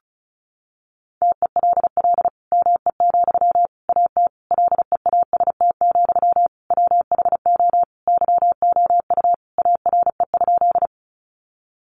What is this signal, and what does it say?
Morse code, heard as a keyed tone, about 34 wpm, 700 Hz. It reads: TELL ME, AT LEAST, WHO YOU ARE?
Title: Sentences from Top 700 Words 35wpm